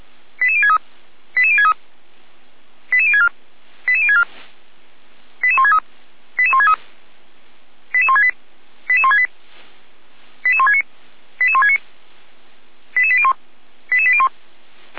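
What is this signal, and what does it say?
Five-tone selective call (ZVEI), heard as quick runs of five tones, tones of about 70 ms. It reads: 88063 88063 88064 88064 88163 88163 88168 88168 88169 88169 88801 88801
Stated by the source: ZVEI five-tone selective call received on an SDR.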